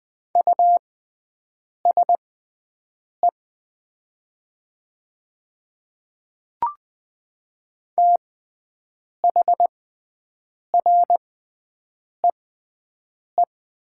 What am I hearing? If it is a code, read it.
Morse code, heard as a keyed tone, about 20 wpm, spaced as for 7 wpm, 700 Hz. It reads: USE THREE
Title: 6x Character Spacing - Top 300 Words - 20wpm